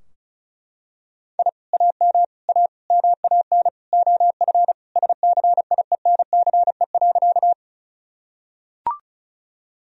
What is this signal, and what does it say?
Morse code, heard as a keyed tone, about 36 wpm, 700 Hz. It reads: I AM A MAN OF SCIENCE.
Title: Sentences from Top 1000 Words - Review Part 2 - 35wpm